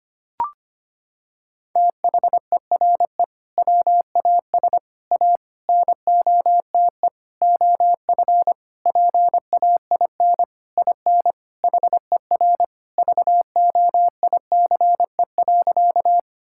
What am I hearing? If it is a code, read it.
Morse code, heard as a keyed tone, about 25 wpm, 700 Hz. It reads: THERE WAS A NOTE OF PAIN IN HER VOICE.
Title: Sentences from Top 1600 Words 25wpm